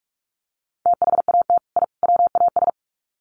Morse code, heard as a keyed tone, about 44 wpm, 700 Hz. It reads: THAT I WAS
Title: Sets of 3 Words 45wpm